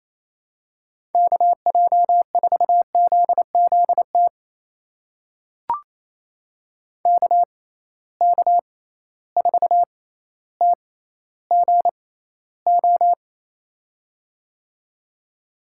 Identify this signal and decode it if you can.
Morse code, heard as a keyed tone, about 28 wpm, spaced as for 14 wpm, 700 Hz. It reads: KJ4ZZT KK4TGO
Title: USA Call Signs - 6 Characters - 6x Character Spacing - 28wpm